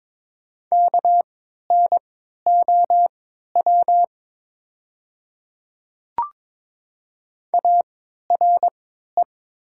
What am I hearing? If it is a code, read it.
Morse code, heard as a keyed tone, about 21 wpm, spaced as for 12 wpm, 700 Hz. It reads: KNOW ARE
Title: 3x Character Spacing - Top 200 Words - 22wpm